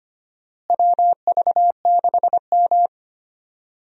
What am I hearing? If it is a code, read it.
Morse code, heard as a keyed tone, about 25 wpm, 700 Hz. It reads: WV6M
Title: USA Call Signs - 4 Characters - Repeated 3x - 25wpm